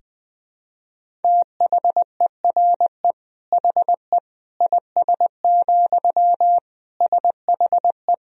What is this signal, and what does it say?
Morse code, heard as a keyed tone, about 21 wpm, 700 Hz. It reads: THERE HE IS, SHE
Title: Sentences from Top 100 Words and Abbreviations 20wpm